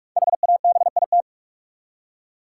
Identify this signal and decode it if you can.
Morse code, heard as a keyed tone, about 46 wpm, 700 Hz. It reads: HABIT
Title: Five Letter Words - 45wpm